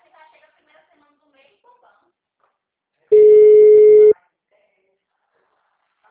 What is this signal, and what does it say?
Ringback tone (Europe, 425 Hz), with speech.
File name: out- -2038- - - .68447